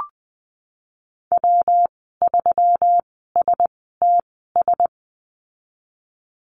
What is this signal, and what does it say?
Morse code, heard as a keyed tone, about 20 wpm, spaced as for 15 wpm, 700 Hz. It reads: W3STS
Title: 140 Lesson 23g - Letter f - Call Signs 20wpm